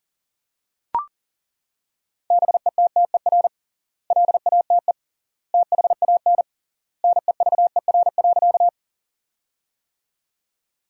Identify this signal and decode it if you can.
Morse code, heard as a keyed tone, about 40 wpm, 700 Hz. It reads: BETTER LATE THAN NEVER.
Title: 3x Word-Spacing - Sentences from Top 400 Words - 40wpm